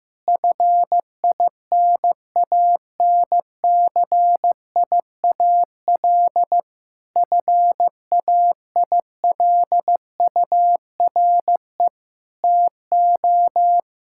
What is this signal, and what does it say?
Morse code, heard as a keyed tone, about 15 wpm, 700 Hz. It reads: FINANCIAL FAILURE TO
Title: Sentences from Top 1000 Words - Review Part 6 - 15wpm